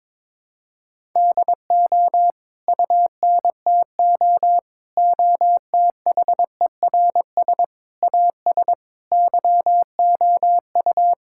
Morse code, heard as a keyed tone, about 22 wpm, 700 Hz. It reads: DO UNTO OTHERS AS YOU
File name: Sets of 9 Words - 22wpm